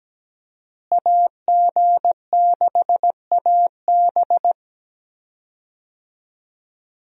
Morse code, heard as a keyed tone, about 17 wpm, 700 Hz. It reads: AG6AB
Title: USA Call Signs - 5 Characters - Repeated 2x - 17wpm